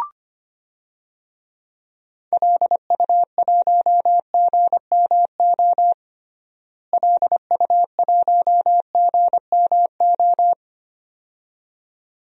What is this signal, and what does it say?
Morse code, heard as a keyed tone, about 25 wpm, 700 Hz. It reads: LU1GMO LU1GMO
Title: DXCC-100 Call Signs - 6 Characters - Repeated 2x - 25wpm